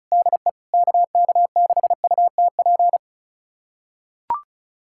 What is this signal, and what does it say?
Morse code, heard as a keyed tone, about 34 wpm, 700 Hz. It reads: DE KK6UTP
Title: USA Callsign DE Callsign - 6 Characters - 35wpm